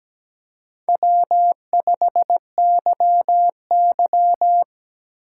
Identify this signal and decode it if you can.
Morse code, heard as a keyed tone, about 19 wpm, 700 Hz. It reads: W5YY
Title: Speed Racing - USA 1x2 Call Signs 17wpm